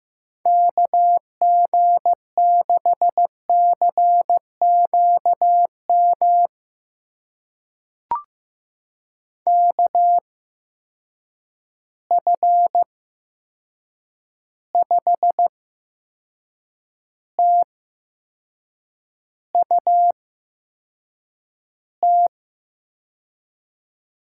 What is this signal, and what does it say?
Morse code, heard as a keyed tone, about 15 wpm, spaced as for 6 wpm, 700 Hz. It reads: KG6CQM KF5TUT
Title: USA Call Signs - 8x Character Spacing - 15wpm